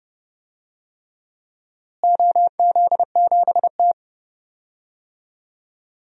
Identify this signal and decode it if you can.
Morse code, heard as a keyed tone, about 30 wpm, 700 Hz. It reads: OZ7T
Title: DXCC-100 Call Signs - 4 Characters - Repeated 3x - 30wpm